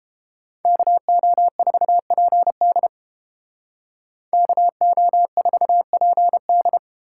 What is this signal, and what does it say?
Morse code, heard as a keyed tone, about 29 wpm, 700 Hz. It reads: KO4PD KO4PD